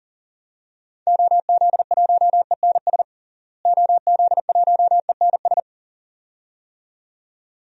Morse code, heard as a keyed tone, about 40 wpm, 700 Hz. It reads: OZ1ENS OZ1ENS
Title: DXCC-100 Call Signs - 6 Characters - Repeated 2x - 40wpm